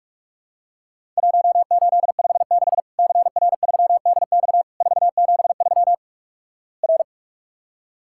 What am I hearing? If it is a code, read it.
Morse code, heard as a keyed tone, about 44 wpm, 700 Hz. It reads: 1856 CR3DX 473 R
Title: Worked All Europe DX Contest - QTC Exchange - 45wpm